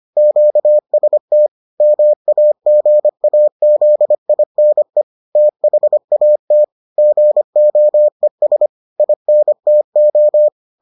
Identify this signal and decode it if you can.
Morse code, heard as a keyed tone, about 25 wpm, 600 Hz. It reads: QST MAGAZINE THAT GOES INTO